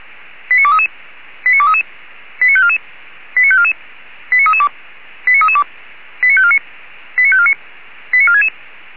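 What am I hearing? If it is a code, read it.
Five-tone selective call (ZVEI), heard as quick runs of five tones, tones of about 70 ms. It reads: 89233 89233 89544 89544 89332 89332 89549 89549 89466
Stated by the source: Five-tone ZVEI selective call received by SDR, from feuerwehr-orlamuende.org fire brigade audio archive.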